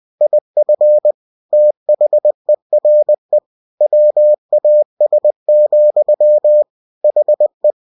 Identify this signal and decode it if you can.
Morse code, heard as a keyed tone, about 20 wpm, 600 Hz. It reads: IF THERE WAS, HE
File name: cw-books / winnie-the-pooh / winnie-20wpm-page-0041